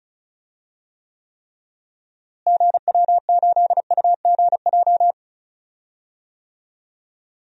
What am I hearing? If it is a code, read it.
Morse code, heard as a keyed tone, about 36 wpm, 700 Hz. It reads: GW8UGJ
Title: DXCC-100 Call Signs - 6 Characters - Repeated 3x - 35wpm